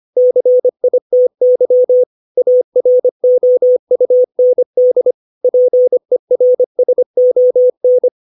Morse code, heard as a keyed tone, about 25 wpm, 500 Hz. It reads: CITY AROUND PERSON
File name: 632 words / 632 Words 25WPM